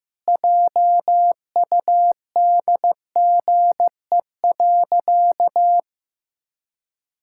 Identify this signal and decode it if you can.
Morse code, heard as a keyed tone, about 15 wpm, 700 Hz. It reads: JUDGE.
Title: Pangram Challenge v2 - 15wpm